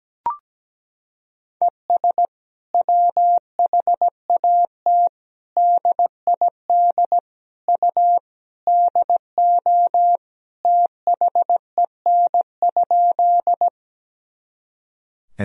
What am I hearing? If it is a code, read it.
Morse code, heard as a keyed tone, about 17 wpm, 700 Hz. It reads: ES WHAT DID U DO THEN?
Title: Sentences from Top 100 Words and Abbreviations 17wpm